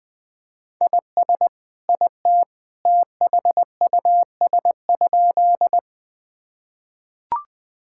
Morse code, heard as a keyed tone, about 20 wpm, 700 Hz. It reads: IS IT THUS?
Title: Sentences from Top 1000 Words - Review Part 5 - 20wpm